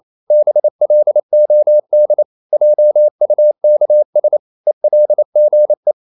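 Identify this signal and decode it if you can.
Morse code, heard as a keyed tone, about 28 wpm, 600 Hz. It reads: BLOD JUKS ELGE
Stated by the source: Danish short words sent in Morse at 28 wpm, from CW med Gnister.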